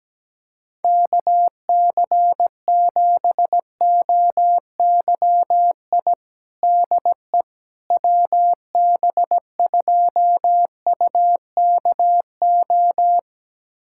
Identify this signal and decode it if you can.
Morse code, heard as a keyed tone, about 17 wpm, 700 Hz. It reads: KC7OYI DE WB2UKO
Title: USA Callsign DE Callsign - 6 Characters - 17wpm